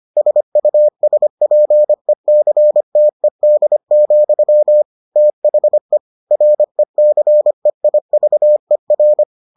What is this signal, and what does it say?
Morse code, heard as a keyed tone, about 25 wpm, 600 Hz. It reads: SUSPECTED, THE RECEIVER